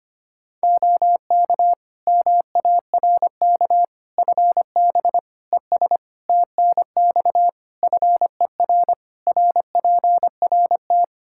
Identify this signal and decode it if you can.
Morse code, heard as a keyed tone, about 25 wpm, 700 Hz. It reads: OK MARK FB ES TNX FER RPRT
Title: Practice QSO - Exchange 2 - Info - 25wpm